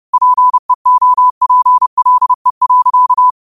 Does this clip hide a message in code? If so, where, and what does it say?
Yes: Morse code from 0.1 to 3.3 s, heard as a keyed tone, about 30 wpm, 1000 Hz. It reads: PEOPLE.